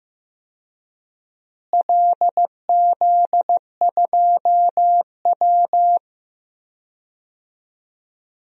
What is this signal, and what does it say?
Morse code, heard as a keyed tone, about 15 wpm, 700 Hz. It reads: LZ2W
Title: DXCC-100 Call Signs - 4 Characters - Repeated 2x - 15wpm